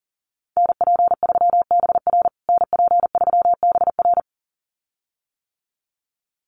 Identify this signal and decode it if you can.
Morse code, heard as a keyed tone, about 40 wpm, 700 Hz. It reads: NP3BR NP3BR
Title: USA Call Signs Repeated 2x 40wpm